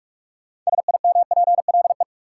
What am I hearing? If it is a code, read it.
Morse code, heard as a keyed tone, about 41 wpm, 700 Hz. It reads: SIMPLE
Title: Sentences from Top 400 Words - New Word Drill - 45wpm